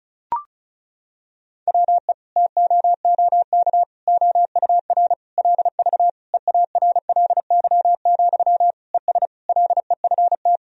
Morse code, heard as a keyed tone, about 36 wpm, 700 Hz. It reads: WE TOOK OUR LV EARLY, ES LEFT
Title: Sentences from Top 400 Words and Abbreviations 35wpm